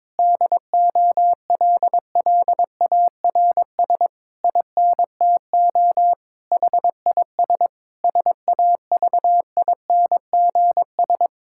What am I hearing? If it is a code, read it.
Morse code, heard as a keyed tone, about 22 wpm, 700 Hz. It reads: DOLLARS INTO HIS SAVINGS
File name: Difficult Sentences with Hints - 22wpm